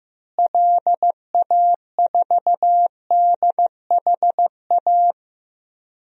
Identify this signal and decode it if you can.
Morse code, heard as a keyed tone, about 16 wpm, 700 Hz. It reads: LA4DHA
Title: DXCC-100 Call Signs - 6 Characters - Repeated 3x - 15wpm